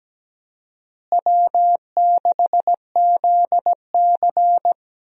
Morse code, heard as a keyed tone, about 17 wpm, 700 Hz. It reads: W6ZC
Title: USA Call Signs - 4 Characters - Repeated 2x - 17wpm